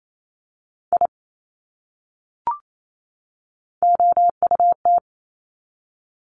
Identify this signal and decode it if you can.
Morse code, heard as a keyed tone, about 28 wpm, 700 Hz. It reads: I OUT
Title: Top 200 Words 28wpm